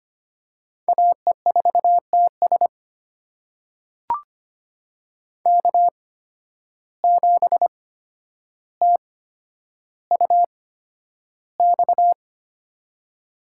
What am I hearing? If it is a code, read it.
Morse code, heard as a keyed tone, about 25 wpm, spaced as for 11 wpm, 700 Hz. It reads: AE4TS K7TUX
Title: USA Call Signs - 5 Characters - 8x Character Spacing - 25wpm